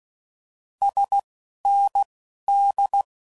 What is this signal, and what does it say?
Morse code, heard as a keyed tone, about 16 wpm, spaced as for 11 wpm, 800 Hz. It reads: SND
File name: Mixed / Abreviations / Abreviations